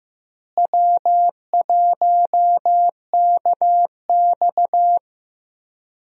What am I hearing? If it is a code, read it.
Morse code, heard as a keyed tone, about 15 wpm, 700 Hz. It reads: W1KX
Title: USA Call Signs - 4 Characters - Repeated 2x - 15wpm